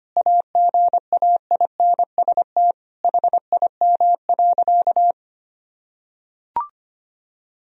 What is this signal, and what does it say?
Morse code, heard as a keyed tone, about 25 wpm, 700 Hz. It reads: AGAINST HIM.
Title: Sentences from Top 1000 Words - Review Part 3 - 25wpm